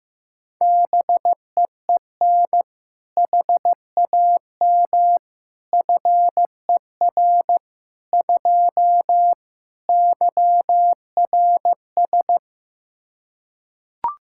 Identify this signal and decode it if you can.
Morse code, heard as a keyed tone, about 15 wpm, 700 Hz. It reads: BEEN HAM FER 2 YRS
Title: K1USN SST QSO Practice 15wpm